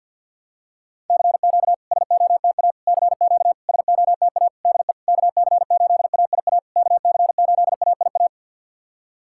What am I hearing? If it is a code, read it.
Morse code, heard as a keyed tone, about 50 wpm, 700 Hz. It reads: CQ SOTA CQ SOTA DE KC8AIA KC8AIA
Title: SOTA QSO Practice 50wpm